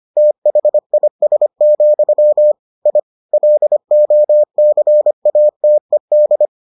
Morse code, heard as a keyed tone, about 25 wpm, 600 Hz. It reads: THIS, I LOCATED